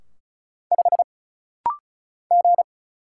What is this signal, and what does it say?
Morse code, heard as a keyed tone, about 33 wpm, 700 Hz. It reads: 5 G
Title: Single Letter-Number Alt - Rapid-Fire 35wpm